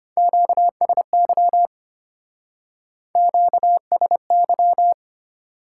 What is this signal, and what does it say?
Morse code, heard as a keyed tone, about 33 wpm, 700 Hz. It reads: QSY QSY